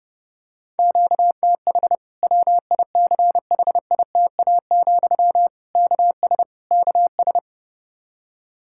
Morse code, heard as a keyed tone, about 30 wpm, 700 Hz. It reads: QTH WICHITA, KS KS
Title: K1USN SST QSO Practice 30wpm